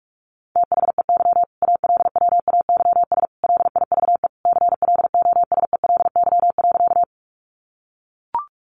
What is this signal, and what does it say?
Morse code, heard as a keyed tone, about 44 wpm, 700 Hz. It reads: THEY ALWAYS LIVE CLOSELY.
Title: Sentences from Top 1400 Words 45wpm